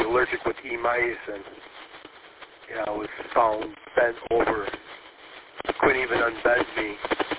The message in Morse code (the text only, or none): none